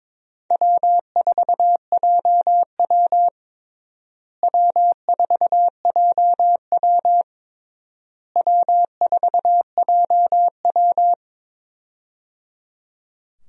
Morse code, heard as a keyed tone, about 22 wpm, 700 Hz. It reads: W4JW W4JW W4JW
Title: USA Call Signs - 4 Characters - Repeated 3x - 22wpm